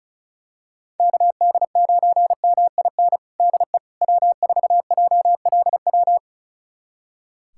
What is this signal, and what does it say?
Morse code, heard as a keyed tone, about 36 wpm, 700 Hz. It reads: KD9MIN DE W4JLW
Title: USA Callsign DE Callsign 35wpm